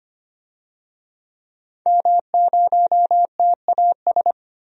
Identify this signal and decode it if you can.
Morse code, heard as a keyed tone, about 25 wpm, 700 Hz. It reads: M0TAS
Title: DXCC-100 Call Signs - 5 Characters - 25wpm